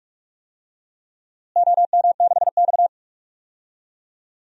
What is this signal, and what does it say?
Morse code, heard as a keyed tone, about 46 wpm, 700 Hz. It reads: OM6X
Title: DXCC-100 Call Signs - 4 Characters - Repeated 2x - 45wpm